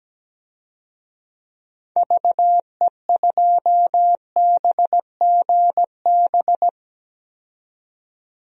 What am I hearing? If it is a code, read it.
Morse code, heard as a keyed tone, about 17 wpm, 700 Hz. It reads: VE2BGB
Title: DXCC-100 Call Signs - 6 Characters - 17wpm